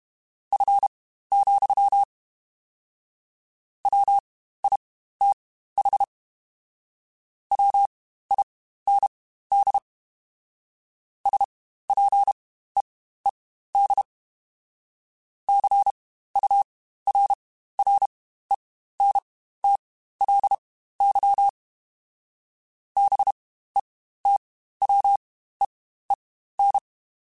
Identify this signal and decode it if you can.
Morse code, heard as a keyed tone, about 32 wpm, spaced as for 15 wpm, 800 Hz. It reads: F, WITH WIND SPEED CURRENTLY BETWEEN
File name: Spaced Characters / Spaced Characters-mp3s / Weather5 22 sp @30